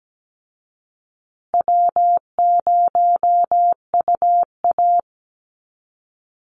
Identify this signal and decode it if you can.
Morse code, heard as a keyed tone, about 17 wpm, 700 Hz. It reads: W0UA
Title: Easy Call Signs Repeated 2x - 17wpm